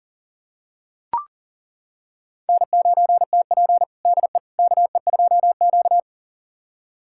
Morse code, heard as a keyed tone, about 40 wpm, 700 Hz. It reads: N9TP DE KE2Q
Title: USA Callsign DE Callsign - 4 Characters - 40wpm